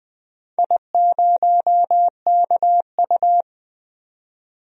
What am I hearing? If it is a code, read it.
Morse code, heard as a keyed tone, about 20 wpm, 700 Hz. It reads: I0KU
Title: DXCC-100 Call Signs - 4 Characters - 20wpm